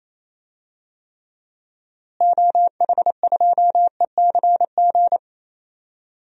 Morse code, heard as a keyed tone, about 28 wpm, 700 Hz. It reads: OH2ECG
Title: DXCC-100 Call Signs - 6 Characters - Repeated 3x - 28wpm